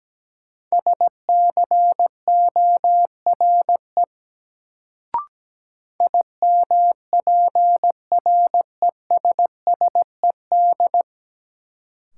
Morse code, heard as a keyed tone, about 17 wpm, 700 Hz. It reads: SCORE IMPRESSED
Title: Sentences from Top 1600 Words - New Word Drill - 17wpm